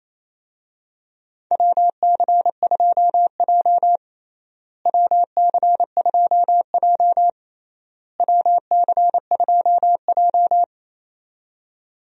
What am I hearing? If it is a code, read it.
Morse code, heard as a keyed tone, about 28 wpm, 700 Hz. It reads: WC2J WC2J WC2J
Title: USA Call Signs - 4 Characters - Repeated 3x - 28wpm